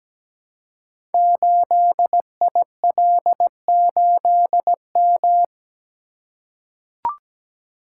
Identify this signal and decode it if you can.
Morse code, heard as a keyed tone, about 17 wpm, 700 Hz. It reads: 8IL8M
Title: Code Groups - 5 Letter:Numbers - 17wpm